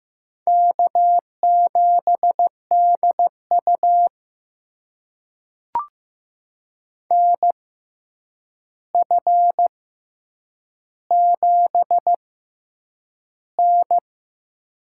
Morse code, heard as a keyed tone, about 15 wpm, spaced as for 8 wpm, 700 Hz. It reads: K7DU NF7N
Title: USA Call Signs - 4 Characters - 6x Character Spacing - 15wpm